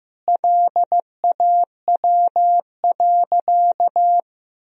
Morse code, heard as a keyed tone, about 15 wpm, 700 Hz. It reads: LAW.
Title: Sentences from Top 1000 Words - Review Part 4 - 15wpm